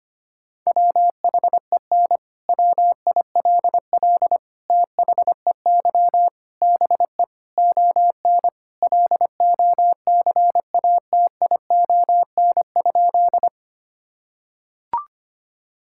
Morse code, heard as a keyed tone, about 25 wpm, 700 Hz. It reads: WHEN WILL THEY BE ON LOCATION?
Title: Sentences from Top 1000 Words - Review Part 6 - 25wpm